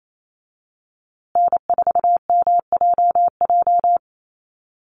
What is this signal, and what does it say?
Morse code, heard as a keyed tone, about 28 wpm, 700 Hz. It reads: N4MJJ
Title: USA Call Signs Repeated 3x 28wpm